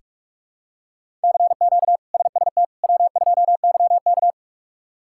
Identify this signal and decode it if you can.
Morse code, heard as a keyed tone, about 46 wpm, 700 Hz. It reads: CQ SST W2YK
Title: K1USN SST QSO Practice - 45wpm